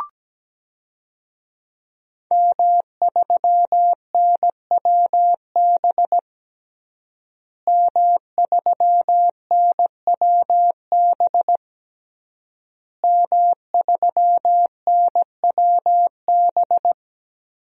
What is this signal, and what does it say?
Morse code, heard as a keyed tone, about 17 wpm, 700 Hz. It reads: M3NWB M3NWB M3NWB
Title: DXCC-100 Call Signs - 5 Characters - Repeated 3x - 17wpm